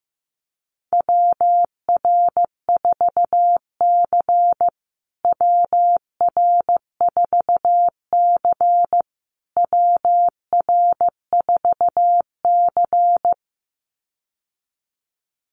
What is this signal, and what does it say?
Morse code, heard as a keyed tone, about 15 wpm, 700 Hz. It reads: WR4C WR4C WR4C
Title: USA Call Signs Repeated 3x 15wpm